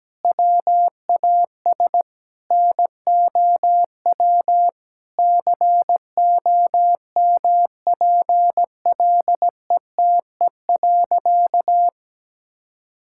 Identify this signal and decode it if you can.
Morse code, heard as a keyed tone, about 17 wpm, 700 Hz. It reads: WAS NOW COMPLETE.
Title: Sentences from Top 1000 Words - Review Part 6 - 17wpm